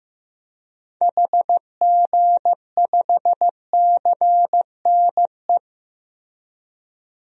Morse code, heard as a keyed tone, about 15 wpm, 700 Hz. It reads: HG5CNE
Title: DXCC-100 Call Signs - 6 Characters - Repeated 2x - 15wpm